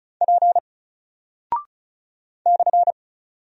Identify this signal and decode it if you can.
Morse code, heard as a keyed tone, about 36 wpm, 700 Hz. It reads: P /
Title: Single Character (Letter:Number:Punctuation:Prosign) Alt 35wpm